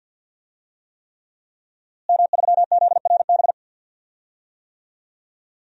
Morse code, heard as a keyed tone, about 52 wpm, 700 Hz. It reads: M3ZRB
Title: DXCC-100 Call Signs - 5 Characters - Repeated 3x - 50wpm